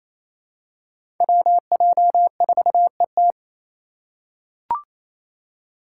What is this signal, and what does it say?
Morse code, heard as a keyed tone, about 28 wpm, 700 Hz. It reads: WJ4ET
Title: USA Call Signs - 5 Characters - 2x Character Spacing - 28wpm